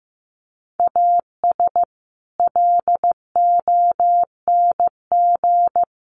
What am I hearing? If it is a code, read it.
Morse code, heard as a keyed tone, about 15 wpm, 700 Hz. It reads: AS LONG
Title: Sets of 4 Words 15wpm